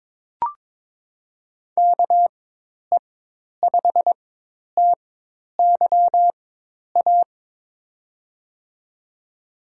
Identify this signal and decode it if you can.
Morse code, heard as a keyed tone, about 22 wpm, spaced as for 10 wpm, 700 Hz. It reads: KE5TYA
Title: USA Call Signs - 4x Character Spacing - 22wpm